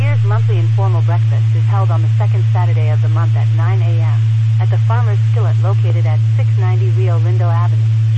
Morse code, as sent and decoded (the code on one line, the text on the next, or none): none